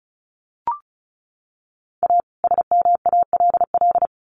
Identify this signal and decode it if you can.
Morse code, heard as a keyed tone, about 34 wpm, 700 Hz. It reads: A SMALL
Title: Sets of 2 Words 35wpm